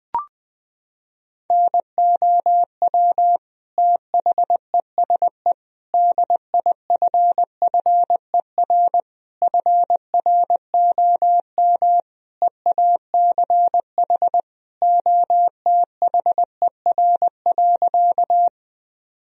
Morse code, heard as a keyed tone, about 21 wpm, 700 Hz. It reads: NOW THESE DIFFER FROM EACH OTHER.
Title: Sentences from Top 1000 Words - Review Part 1 - 20wpm